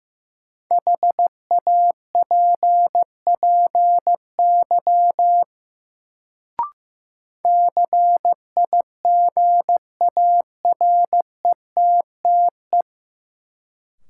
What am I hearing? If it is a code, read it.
Morse code, heard as a keyed tone, about 15 wpm, 700 Hz. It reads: HAPPY CIGARETTE
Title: Sentences from Top 1000 Words - New Word Drill - 15wpm